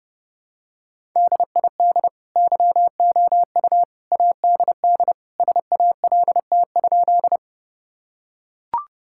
Morse code, heard as a keyed tone, about 32 wpm, 700 Hz. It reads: DID YOU ADD SALT?
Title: Sentences from Top 1000 Words - Review Part 5 - 30wpm